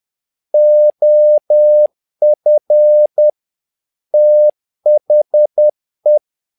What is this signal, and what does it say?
Morse code, heard as a keyed tone, about 10 wpm, 600 Hz. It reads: OF THE
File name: learning center / tower trouble / tower trouble 10wpm